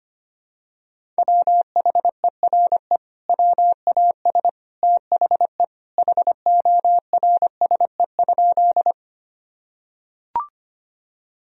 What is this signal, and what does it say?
Morse code, heard as a keyed tone, about 25 wpm, 700 Hz. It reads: WHERE WAS THE HORSE?
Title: Sentences from Top 1000 Words - Review Part 1 - 25wpm